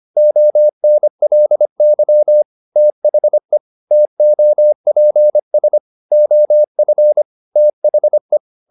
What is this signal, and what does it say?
Morse code, heard as a keyed tone, about 25 wpm, 600 Hz. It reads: ONLY THE TOPS OF THE
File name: the old man and the sea part 1 25wpm